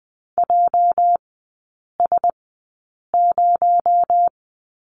Morse code, heard as a keyed tone, about 20 wpm, spaced as for 8 wpm, 700 Hz. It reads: JS0
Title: 218 Lesson 35e - Number 0 - Three Characters 20wpm